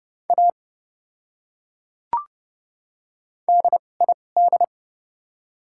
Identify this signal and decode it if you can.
Morse code, heard as a keyed tone, about 30 wpm, 700 Hz. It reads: A D I D
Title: 2x Character Spacing - Top 100 Words - 30wpm